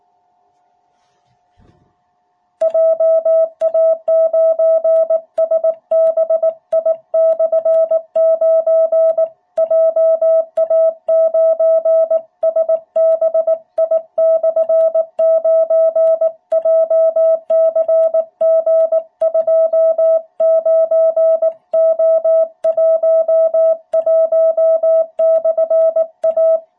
Morse code, heard as a keyed tone, about 18 wpm, 660 Hz. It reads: JA9SBI/9 JA9SBI/9 JCG29O11/A
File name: jh8cbh / 2021cw